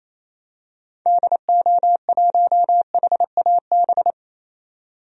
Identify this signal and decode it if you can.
Morse code, heard as a keyed tone, about 28 wpm, 700 Hz. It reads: DO1HAB
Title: DXCC-100 Call Signs - 6 Characters - 28wpm